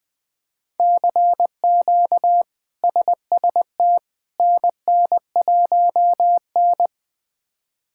Morse code, heard as a keyed tone, about 21 wpm, 700 Hz. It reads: CQ SST NN1N
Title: K1USN SST QSO Practice - 20wpm